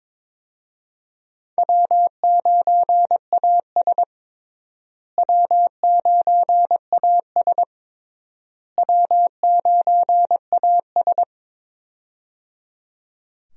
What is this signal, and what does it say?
Morse code, heard as a keyed tone, about 21 wpm, 700 Hz. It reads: W9AS W9AS W9AS
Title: USA Call Signs - 4 Characters - Repeated 3x - 22wpm